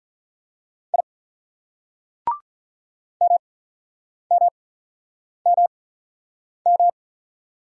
Morse code, heard as a keyed tone, about 47 wpm, spaced as for 9 wpm, 700 Hz. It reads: I MMMM